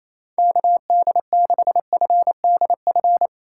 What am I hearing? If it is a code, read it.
Morse code, heard as a keyed tone, about 28 wpm, 700 Hz. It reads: KD6FDF
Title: USA Call Signs - 4x Character Spacing - 28wpm